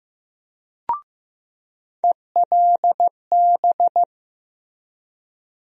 Morse code, heard as a keyed tone, about 15 wpm, 700 Hz. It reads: ELB
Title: USA Call Sign Postfixes - 3 Letters 15wpm